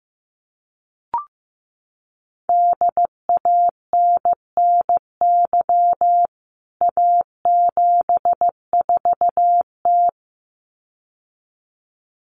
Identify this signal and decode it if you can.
Morse code, heard as a keyed tone, about 15 wpm, 700 Hz. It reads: DANNY A74T
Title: CWT Exchange 15wpm